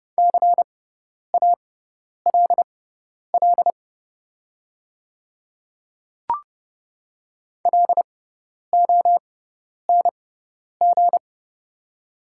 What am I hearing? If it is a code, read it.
Morse code, heard as a keyed tone, about 29 wpm, spaced as for 10 wpm, 690 Hz. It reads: CALL LONG
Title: 6x Character Spacing - Top 100 Words - 30wpm